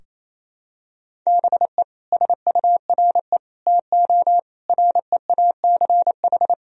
Morse code, heard as a keyed tone, about 28 wpm, 700 Hz. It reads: BE SURE TO REACH
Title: Sentences from Top 1000 Words - Review Part 2 - 28wpm